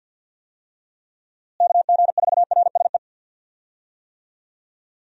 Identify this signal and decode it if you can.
Morse code, heard as a keyed tone, about 50 wpm, 700 Hz. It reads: KG4RSE
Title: USA Call Signs - 6 Characters - Repeated 2x - 50wpm